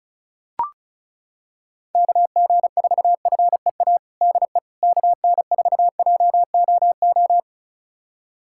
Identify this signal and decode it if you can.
Morse code, heard as a keyed tone, about 36 wpm, 700 Hz. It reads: KG4FEA DE KN4JOO
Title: USA Callsign DE Callsign - 6 Characters - 35wpm